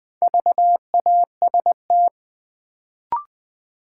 Morse code, heard as a keyed tone, about 20 wpm, 700 Hz. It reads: VAST
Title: Sentences from Top 1300 Words - New Word Drill - 20wpm